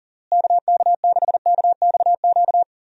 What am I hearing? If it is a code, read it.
Morse code, heard as a keyed tone, about 40 wpm, 700 Hz. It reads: KK6KXQ
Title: USA Call Signs - 6 Characters - Repeated 3x - 40wpm